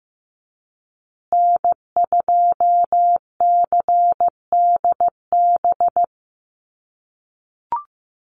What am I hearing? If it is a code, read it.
Morse code, heard as a keyed tone, about 15 wpm, 700 Hz. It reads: N2CDB
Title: USA Call Signs Repeated 2x 15wpm